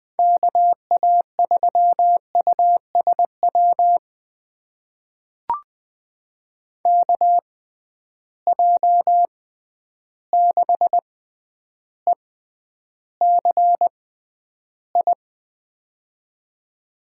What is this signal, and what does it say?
Morse code, heard as a keyed tone, about 20 wpm, spaced as for 10 wpm, 700 Hz. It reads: KA3USW KJ6ECI
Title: USA Call Signs - 6 Characters - 6x Character Spacing - 20wpm